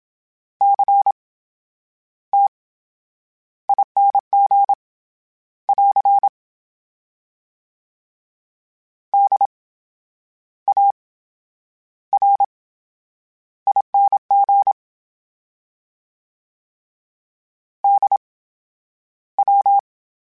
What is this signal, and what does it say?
Morse code, heard as a keyed tone, about 27 wpm, spaced as for 9 wpm, 800 Hz. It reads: CTING <AR> DARING DW
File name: Special INGs / Special INGs-mp3s / VerySlowToFaster Ing @25